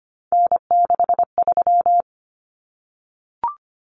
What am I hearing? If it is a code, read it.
Morse code, heard as a keyed tone, about 25 wpm, 700 Hz. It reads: N63
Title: Common Cut Numbers - 3-Digit 25wpm